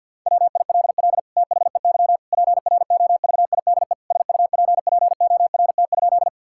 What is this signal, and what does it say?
Morse code, heard as a keyed tone, about 50 wpm, 700 Hz. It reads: WILL THEY PROVIDE SUPPORT?
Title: Sentences from Top 1000 Words - Review Part 5 - 50wpm